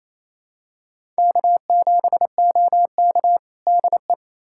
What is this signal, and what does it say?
Morse code, heard as a keyed tone, about 28 wpm, 700 Hz. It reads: K7OK DE
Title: USA Callsign DE Callsign - 4 Characters - 28wpm